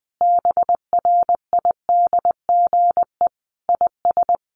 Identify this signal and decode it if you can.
Morse code, heard as a keyed tone, about 20 wpm, 700 Hz. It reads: BRIDGE IS
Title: Sentences from Top 1400 Words 20wpm